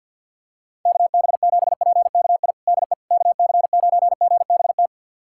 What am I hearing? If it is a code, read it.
Morse code, heard as a keyed tone, about 48 wpm, 700 Hz. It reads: KB7PKI DE KC9GBT
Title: USA Callsign DE Callsign - 6 Characters - 50wpm